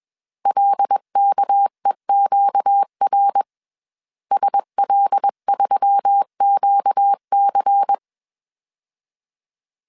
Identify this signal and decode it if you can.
Morse code, heard as a keyed tone, about 21 wpm, 800 Hz. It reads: LKEQR SL3QC
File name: CW / DET / 1 6 1500C S C 20wpm 2116